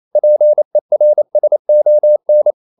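Morse code, heard as a keyed tone, about 28 wpm, 600 Hz. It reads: PERSON
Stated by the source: Morse code practice at 28 wpm, text of Ken Follett's novel Never in English, from CW med Gnister.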